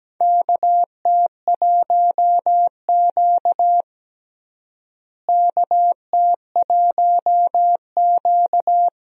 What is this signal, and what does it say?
Morse code, heard as a keyed tone, about 17 wpm, 700 Hz. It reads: KT1Q KT1Q
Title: USA Call Signs - 4 Characters - Repeated 3x - 17wpm